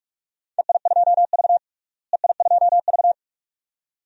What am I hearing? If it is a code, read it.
Morse code, heard as a keyed tone, about 41 wpm, 700 Hz. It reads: EI2V EI2V
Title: DXCC-100 Call Signs - 4 Characters - Repeated 2x - 45wpm